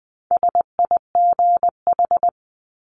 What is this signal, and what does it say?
Morse code, heard as a keyed tone, about 20 wpm, 700 Hz. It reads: SIGH
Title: 260 Lesson 42e - Words 20-20wpm